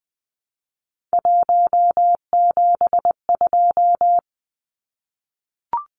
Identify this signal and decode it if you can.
Morse code, heard as a keyed tone, about 20 wpm, 700 Hz. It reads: 172
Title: Numbers - 3-Digit 20wpm